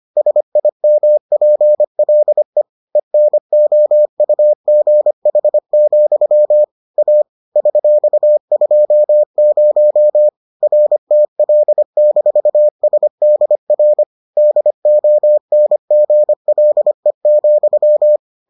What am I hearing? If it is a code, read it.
Morse code, heard as a keyed tone, about 25 wpm, 600 Hz. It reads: SIMPLE ENOUGH, A $20 RTL-SDR DONGLE,